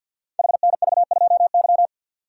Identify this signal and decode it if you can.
Morse code, heard as a keyed tone, about 48 wpm, 700 Hz. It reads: HNV2Y
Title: Code Groups - 5 Letter:Numbers - 50wpm